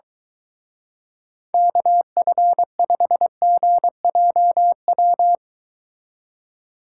Morse code, heard as a keyed tone, about 24 wpm, 700 Hz. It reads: KF5GJW